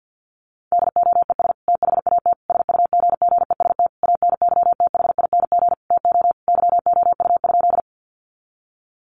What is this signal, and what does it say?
Morse code, heard as a keyed tone, about 48 wpm, 700 Hz. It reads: DOES THAT SUGGEST ANYTHING TO YOU?
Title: Sentences from Top 1300 Words 50wpm